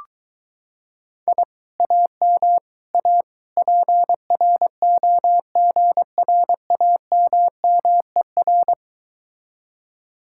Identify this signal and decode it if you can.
Morse code, heard as a keyed tone, about 23 wpm, 700 Hz. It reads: I AM A PROGRAMMER